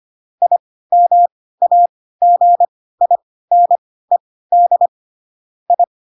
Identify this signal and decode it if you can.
Morse code, heard as a keyed tone, about 25 wpm, spaced as for 16 wpm, 700 Hz. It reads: IMAGINED I
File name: Pinocchio / Pinocchio ch1 10 25 ik1hge